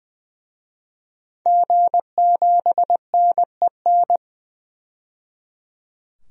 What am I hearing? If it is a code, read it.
Morse code, heard as a keyed tone, about 21 wpm, 700 Hz. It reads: G7NEN
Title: DXCC-100 Call Signs - 5 Characters - 20wpm